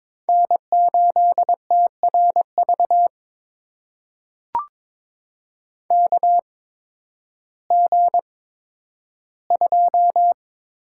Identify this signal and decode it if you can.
Morse code, heard as a keyed tone, about 21 wpm, spaced as for 12 wpm, 700 Hz. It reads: N8TRV KG2
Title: USA Call Signs - 8x Character Spacing - 22wpm